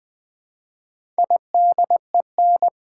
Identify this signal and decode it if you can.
Morse code, heard as a keyed tone, about 20 wpm, 700 Hz. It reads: IDEN
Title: Build A Word - 4 Letters At A Time - 8-Letter Words - 20wpm